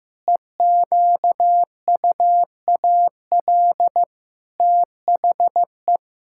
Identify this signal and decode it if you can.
Morse code, heard as a keyed tone, about 15 wpm, 700 Hz. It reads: EQUAL THE
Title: Pangram Challenge v2 - 15wpm